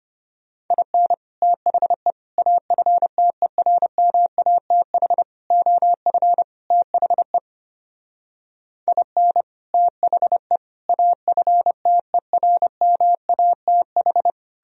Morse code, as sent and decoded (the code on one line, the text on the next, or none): .. -. / - .... / .- ..-. - . .-. -- .- - .... / --- ..-. / - .... / .. -. / - .... / .- ..-. - . .-. -- .- - ....
IN THE AFTERMATH OF THE IN THE AFTERMATH